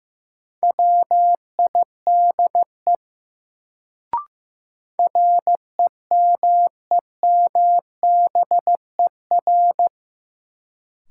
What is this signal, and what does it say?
Morse code, heard as a keyed tone, about 16 wpm, 700 Hz. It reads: WIDE REMEMBER
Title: Sets of 4 Words - New Word Drill - 15wpm